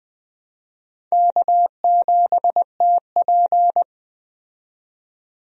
Morse code, heard as a keyed tone, about 20 wpm, 700 Hz. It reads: K7TP
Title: USA Call Signs - 4 Characters - 20wpm